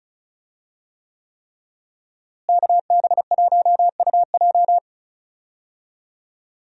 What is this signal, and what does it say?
Morse code, heard as a keyed tone, about 36 wpm, 700 Hz. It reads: KB1UJ
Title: USA Call Signs - 5 Characters - Repeated 3x - 35wpm